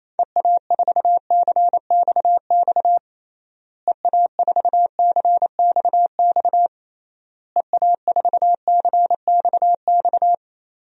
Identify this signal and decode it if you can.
Morse code, heard as a keyed tone, about 28 wpm, 700 Hz. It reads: EA4CXX EA4CXX EA4CXX
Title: DXCC-100 Call Signs - 6 Characters - Repeated 3x - 28wpm